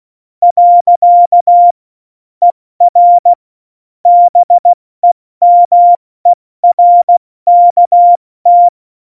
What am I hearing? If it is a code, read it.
Morse code, heard as a keyed tone, about 16 wpm, 700 Hz. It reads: . ER BEMERKT
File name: Morsekurs / Witze / Witz 17. November 2020, 70 F80